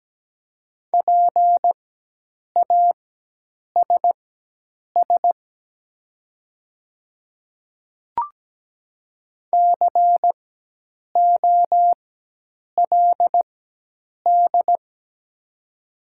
Morse code, heard as a keyed tone, about 17 wpm, spaced as for 8 wpm, 700 Hz. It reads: PASS COLD
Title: 4x Character Spacing - Top 500 Words - 17wpm